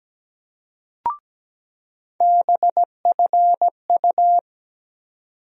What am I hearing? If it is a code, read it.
Morse code, heard as a keyed tone, about 17 wpm, 700 Hz. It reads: BFU
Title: USA Call Sign Postfixes - 3 Letters 17wpm